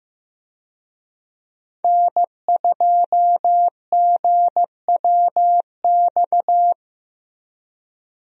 Morse code, heard as a keyed tone, about 15 wpm, 700 Hz. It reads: N2GWX
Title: USA Call Signs - 5 Characters - Repeated 2x - 15wpm